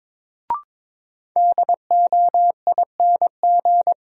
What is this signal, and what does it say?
Morse code, heard as a keyed tone, about 22 wpm, 700 Hz. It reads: DOING
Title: Sets of 4 Words - New Word Drill - 22wpm